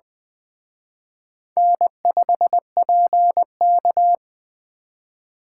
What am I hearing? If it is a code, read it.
Morse code, heard as a keyed tone, about 20 wpm, 700 Hz. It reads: N5PK